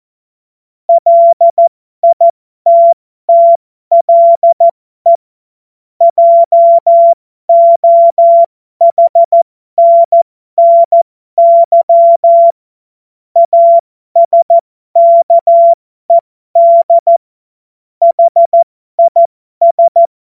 Morse code, heard as a keyed tone, about 14 wpm, spaced as for 12 wpm, 680 Hz. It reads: LITTLE JOHNNY ASKED HIS
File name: Morsekurs / Witze